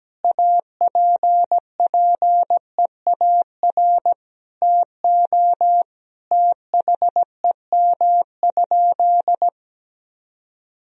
Morse code, heard as a keyed tone, about 17 wpm, 700 Hz. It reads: APPEAR TO THEM?
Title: Sentences from Top 1000 Words - Review Part 2 - 17wpm